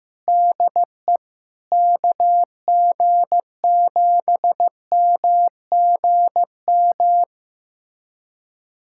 Morse code, heard as a keyed tone, about 15 wpm, 700 Hz. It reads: DE KG7MGM
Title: USA Callsign DE Callsign 15wpm